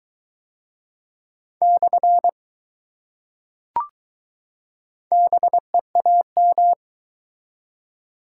Morse code, heard as a keyed tone, about 20 wpm, 700 Hz. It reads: / BEAM